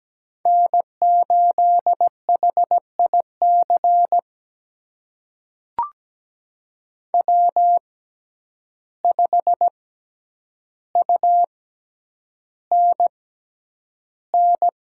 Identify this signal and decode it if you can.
Morse code, heard as a keyed tone, about 17 wpm, spaced as for 9 wpm, 700 Hz. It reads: N8HIC W5UNN
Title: USA Call Signs - 5 Characters - 6x Character Spacing - 17wpm